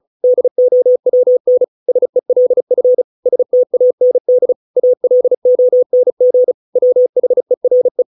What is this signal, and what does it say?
Morse code, heard as a keyed tone, about 34 wpm, 500 Hz. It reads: DOWN SELF STAND ALONG WHERE